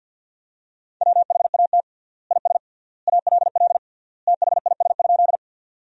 Morse code, heard as a keyed tone, about 52 wpm, 700 Hz. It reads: WHAT IS ALL THIS?
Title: 3x Word-Spacing - Sentences from Top 100 Words - 50wpm